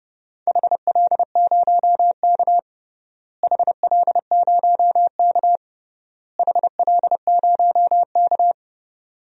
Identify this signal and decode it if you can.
Morse code, heard as a keyed tone, about 30 wpm, 700 Hz. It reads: HL0K HL0K HL0K
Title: DXCC-100 Call Signs - 4 Characters - Repeated 3x - 30wpm